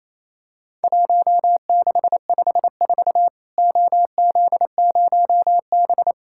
Morse code, heard as a keyed tone, about 28 wpm, 700 Hz. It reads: 1654 OZ0B
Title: Worked All Europe DX Contest - QTC Exchange - 28wpm